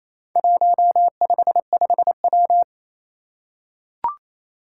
Morse code, heard as a keyed tone, about 28 wpm, 700 Hz. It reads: 155W
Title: Four Challenging Letter-Number Combinations 28wpm